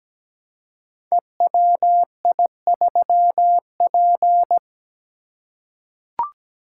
Morse code, heard as a keyed tone, about 17 wpm, 700 Hz. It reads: EWI3P
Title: Code Groups - 5 Letter:Numbers - 17wpm